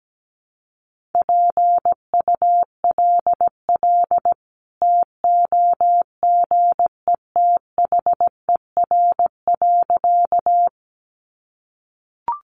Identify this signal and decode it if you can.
Morse code, heard as a keyed tone, about 17 wpm, 700 Hz. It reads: PULL TOGETHER.
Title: Sentences from Top 400 Words 17wpm